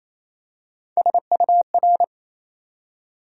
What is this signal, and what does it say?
Morse code, heard as a keyed tone, about 28 wpm, 700 Hz. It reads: SUR
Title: USA Call Sign Postfixes - 3 Letters 28wpm